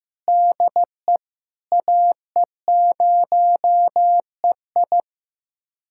Morse code, heard as a keyed tone, about 15 wpm, 700 Hz. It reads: DE AE0EI
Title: USA Callsign DE Callsign - 5 Characters - 15wpm